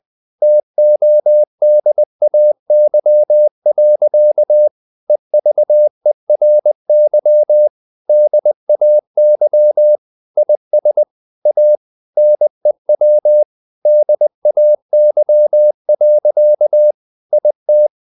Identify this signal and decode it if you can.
Morse code, heard as a keyed tone, about 20 wpm, 600 Hz. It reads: TODAY. EVERY DAY IS A NEW DAY. IT
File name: the old man and the sea / the old man and the sea part 1 20wpm